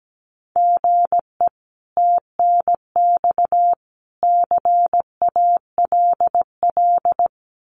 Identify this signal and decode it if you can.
Morse code, heard as a keyed tone, about 17 wpm, 700 Hz. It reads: GE TNX CALL
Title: Common QSO Phrases 17wpm